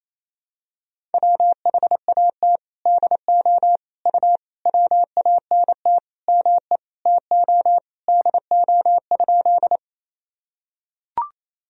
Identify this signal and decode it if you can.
Morse code, heard as a keyed tone, about 28 wpm, 700 Hz. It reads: WHAT DO U WANT ME TO DO?
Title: Sentences from Top 200 Words and Abbreviations 28wpm